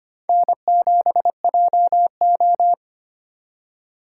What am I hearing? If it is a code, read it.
Morse code, heard as a keyed tone, about 25 wpm, 700 Hz. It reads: N7JO
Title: USA Call Signs - 4 Characters - 3x Character Spacing - 25wpm